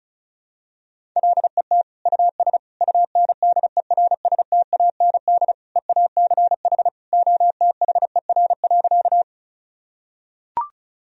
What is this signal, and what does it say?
Morse code, heard as a keyed tone, about 34 wpm, 700 Hz. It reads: LET US UNDERSTAND EACH OTHER.
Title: Sentences from Top 1000 Words - Review Part 1 - 35wpm